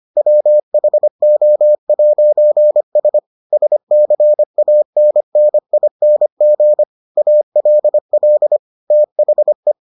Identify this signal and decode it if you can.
Morse code, heard as a keyed tone, about 25 wpm, 600 Hz. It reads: WHO'S SCANNING ALL THE